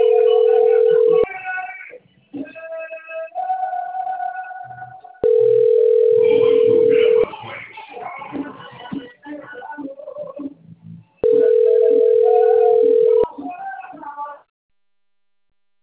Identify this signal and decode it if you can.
Ringback tone (North America), 440 + 480 Hz.